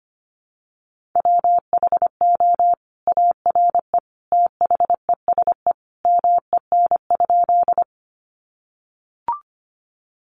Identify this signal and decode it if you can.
Morse code, heard as a keyed tone, about 25 wpm, 700 Hz. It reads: WHO ARE THESE MEN?
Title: Sentences from Top 200 Words 25wpm